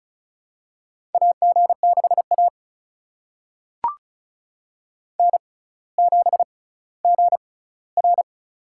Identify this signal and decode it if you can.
Morse code, heard as a keyed tone, about 36 wpm, spaced as for 18 wpm, 700 Hz. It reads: AG6A N7GR
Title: USA Call Signs - 4 Characters - 6x Character Spacing - 35wpm